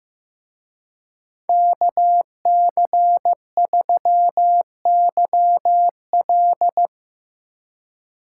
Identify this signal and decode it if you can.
Morse code, heard as a keyed tone, about 18 wpm, 700 Hz. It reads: KC3YL
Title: Speed Racing - USA Call Signs 15wpm